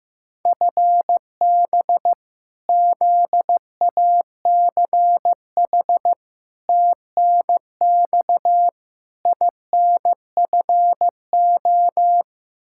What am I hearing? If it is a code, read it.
Morse code, heard as a keyed tone, about 15 wpm, 700 Hz. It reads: FB ZACH TNX INFO
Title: K1USN SST QSO Practice 15wpm